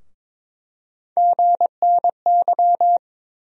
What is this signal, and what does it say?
Morse code, heard as a keyed tone, about 22 wpm, 700 Hz. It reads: GNY
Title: USA Call Sign Postfixes - 3 Letters 22wpm